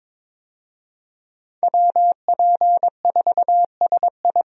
Morse code, heard as a keyed tone, about 22 wpm, 700 Hz. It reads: WP4SI
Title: USA Call Signs - 5 Characters - Repeated 3x - 22wpm